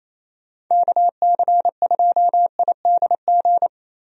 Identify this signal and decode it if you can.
Morse code, heard as a keyed tone, about 28 wpm, 700 Hz. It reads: KC2IDG
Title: USA Call Signs - 6 Characters - Repeated 2x - 28wpm